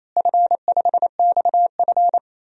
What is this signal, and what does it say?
Morse code, heard as a keyed tone, about 27 wpm, 700 Hz. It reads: F5XF
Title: DXCC-100 Call Signs - 4 Characters - Repeated 3x - 28wpm